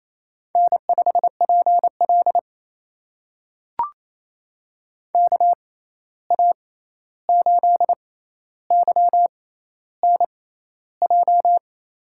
Morse code, heard as a keyed tone, about 28 wpm, spaced as for 13 wpm, 700 Hz. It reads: N5PL KA8YNJ
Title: USA Call Signs - 6x Character Spacing - 28wpm